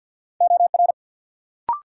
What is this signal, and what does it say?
Morse code, heard as a keyed tone, about 53 wpm, 700 Hz. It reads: OR
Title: Sentences from Top 100 Words - New Word Drill - 50wpm